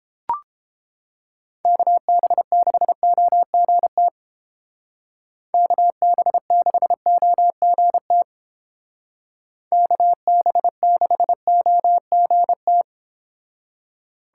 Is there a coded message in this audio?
Yes: Morse code from 1.7 to 12.8 s, heard as a keyed tone, about 30 wpm, 700 Hz. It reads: KB6OGT KB6OGT KB6OGT